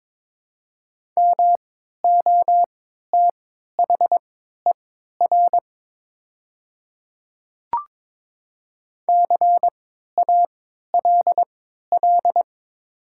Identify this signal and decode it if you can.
Morse code, heard as a keyed tone, about 22 wpm, spaced as for 12 wpm, 700 Hz. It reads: MOTHER CALL
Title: 3x Character Spacing - Top 200 Words - 22wpm